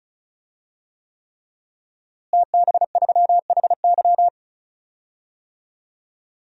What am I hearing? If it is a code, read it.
Morse code, heard as a keyed tone, about 36 wpm, 700 Hz. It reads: TB3HY
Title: DXCC-100 Call Signs - 5 Characters - Repeated 3x - 35wpm